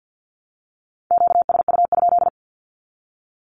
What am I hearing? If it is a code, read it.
Morse code, heard as a keyed tone, about 48 wpm, 700 Hz. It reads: QSU?
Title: Q-Codes Repeated 3x 50wpm